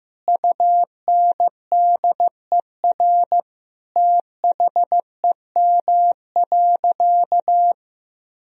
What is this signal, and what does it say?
Morse code, heard as a keyed tone, about 15 wpm, 700 Hz. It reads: UNDER THEM.
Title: Sentences from Top 1000 Words - Review Part 1 - 15wpm